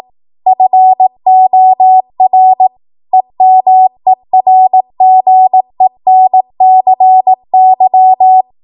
Morse code, heard as a keyed tone, about 18 wpm, 750 Hz. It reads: FOR EMERGENCY
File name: Morse / 18WPM